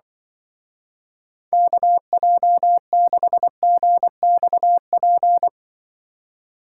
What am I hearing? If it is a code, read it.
Morse code, heard as a keyed tone, about 24 wpm, 700 Hz. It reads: KJ6GXP
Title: Speed Racing - USA 2x3 Call Signs 20wpm